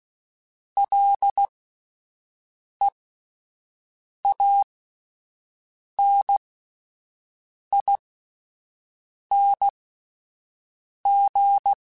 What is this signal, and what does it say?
Morse code, heard as a keyed tone, about 16 wpm, spaced as for 5 wpm, 800 Hz. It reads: LEANING